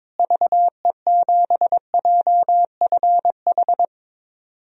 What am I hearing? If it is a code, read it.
Morse code, heard as a keyed tone, about 22 wpm, 700 Hz. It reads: VE7JFH
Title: DXCC-100 Call Signs - 6 Characters - Repeated 2x - 22wpm